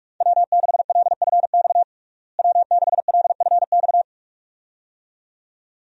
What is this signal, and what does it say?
Morse code, heard as a keyed tone, about 46 wpm, 700 Hz. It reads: W6LFX W6LFX
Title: USA Call Signs - 5 Characters - Repeated 2x - 45wpm